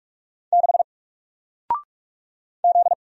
Morse code, heard as a keyed tone, about 44 wpm, 700 Hz. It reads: 6 Z
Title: Single Letter-Number Alt 45wpm